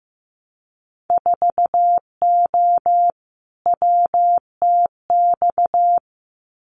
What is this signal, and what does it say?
Morse code, heard as a keyed tone, about 15 wpm, 700 Hz. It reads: 4O WTX
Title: Winter Field Day Exchanges Encore 15wpm